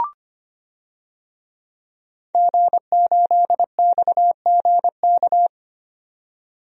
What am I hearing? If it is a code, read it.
Morse code, heard as a keyed tone, about 25 wpm, 700 Hz. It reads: G8XGK
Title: DXCC-100 Call Signs - 5 Characters - 25wpm